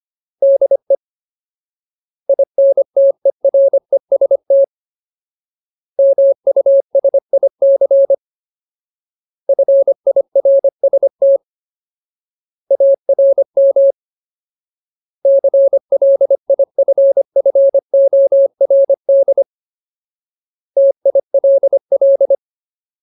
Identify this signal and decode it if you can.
Morse code, heard as a keyed tone, about 25 wpm, 550 Hz. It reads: DE INTEREST MUSIC FIRST ARM CLIFFORD TILL
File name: Ultimate Word List / 25WPM Ultimate